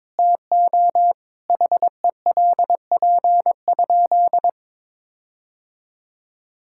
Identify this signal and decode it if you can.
Morse code, heard as a keyed tone, about 22 wpm, 700 Hz. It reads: TO HELP?
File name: practice / 22wpm / Sentences with 4-Word Phrases - 22wpm